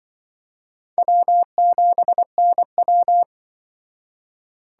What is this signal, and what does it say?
Morse code, heard as a keyed tone, about 25 wpm, 700 Hz. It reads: W7NW
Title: Speed Racing - USA 1x2 Call Signs 20wpm